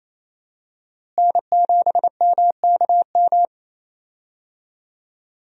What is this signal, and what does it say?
Morse code, heard as a keyed tone, about 28 wpm, 700 Hz. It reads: N7MKM
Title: USA Call Signs - 5 Characters - Repeated 2x - 28wpm